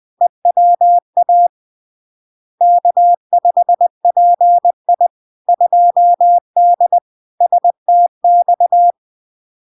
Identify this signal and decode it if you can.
Morse code, heard as a keyed tone, about 20 wpm, 700 Hz. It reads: EWA K5PI 2D STX
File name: LARK-FD-2022-Log-in-CW-20wpm-700-Hz